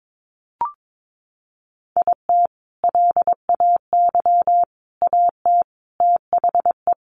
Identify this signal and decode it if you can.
Morse code, heard as a keyed tone, about 22 wpm, 700 Hz. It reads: IT LAY AT THE
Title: Sentences from Top 1500 Words 22wpm